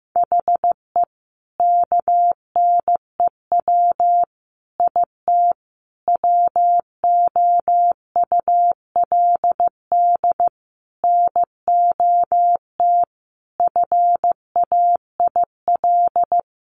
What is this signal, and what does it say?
Morse code, heard as a keyed tone, about 15 wpm, 700 Hz. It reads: HE KNEW IT WOULD NOT FAIL
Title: Sentences from Top 1100 Words 15wpm